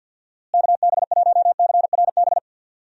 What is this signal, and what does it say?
Morse code, heard as a keyed tone, about 47 wpm, 700 Hz. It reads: KB1CRB
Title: USA Call Signs - 6 Characters - Repeated 2x - 50wpm